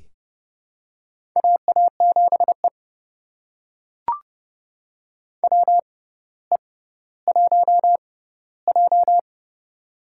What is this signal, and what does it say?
Morse code, heard as a keyed tone, about 30 wpm, spaced as for 15 wpm, 700 Hz. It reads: AA7E WE1J
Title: USA Call Signs - 4 Characters - 6x Character Spacing - 30wpm